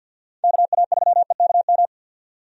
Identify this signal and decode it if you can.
Morse code, heard as a keyed tone, about 51 wpm, 700 Hz. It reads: KA3EKM
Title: USA Call Signs - 6 Characters - 50wpm